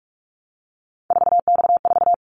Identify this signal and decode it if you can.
Morse code, heard as a keyed tone, about 44 wpm, 700 Hz. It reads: 4X4
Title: Callsign Prefixes for Top 50 DXCC - 45wpm